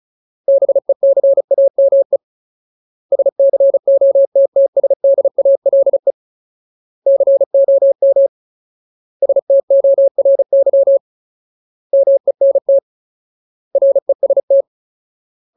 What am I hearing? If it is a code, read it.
Morse code, heard as a keyed tone, about 34 wpm, 550 Hz. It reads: BECAME SCOTTSDALE COM STORY MENT REST
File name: Ultimate Word List / 35WPM Ultimate